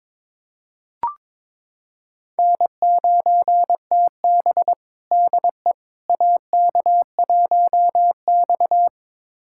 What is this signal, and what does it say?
Morse code, heard as a keyed tone, about 22 wpm, 700 Hz. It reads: N9TB DE AK1X
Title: USA Callsign DE Callsign - 4 Characters - 22wpm